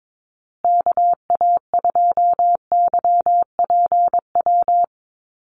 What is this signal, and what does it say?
Morse code, heard as a keyed tone, about 22 wpm, 700 Hz. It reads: KA2YPW
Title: USA Call Signs Repeated 3x 22wpm